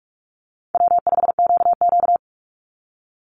Morse code, heard as a keyed tone, about 44 wpm, 700 Hz. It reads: W5QQ
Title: Easy Call Signs - 45wpm